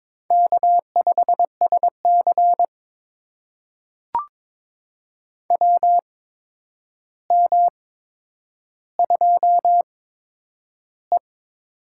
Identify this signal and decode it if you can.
Morse code, heard as a keyed tone, about 22 wpm, spaced as for 9 wpm, 700 Hz. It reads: K5SC WM2E
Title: USA Call Signs - 4 Characters - 8x Character Spacing - 22wpm